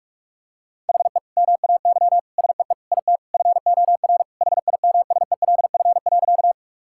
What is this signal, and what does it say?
Morse code, heard as a keyed tone, about 41 wpm, 700 Hz. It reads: HE MAY SEE IT FOR HIMSELF.
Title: Sentences from Top 1600 Words 45wpm